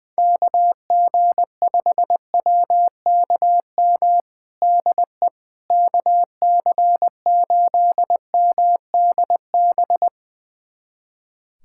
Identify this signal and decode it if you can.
Morse code, heard as a keyed tone, about 20 wpm, 700 Hz. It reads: KG5WKM DE KC8MDB
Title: USA Callsign DE Callsign 20wpm